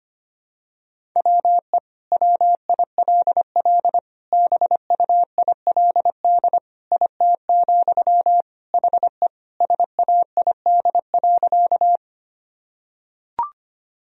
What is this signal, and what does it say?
Morse code, heard as a keyed tone, about 25 wpm, 700 Hz. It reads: WE WILL BUILD IT, HE SAID.
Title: Sentences from Top 1000 Words - Review Part 1 - 25wpm